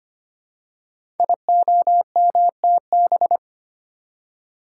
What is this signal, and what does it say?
Morse code, heard as a keyed tone, about 25 wpm, 700 Hz. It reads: IOMTB
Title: Code Groups - 5 Letters - 25wpm